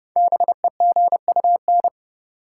Morse code, heard as a keyed tone, about 30 wpm, 700 Hz. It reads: BEGUN
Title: Sentences from Top 1000 Words - Review Part 6 - New Word Drill - 30wpm